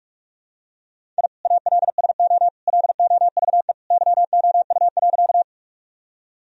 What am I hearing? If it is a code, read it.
Morse code, heard as a keyed tone, about 44 wpm, 700 Hz. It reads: I ALSO LOVE YOU.
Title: Sentences from Top 1000 Words - Review Part 1 - 45wpm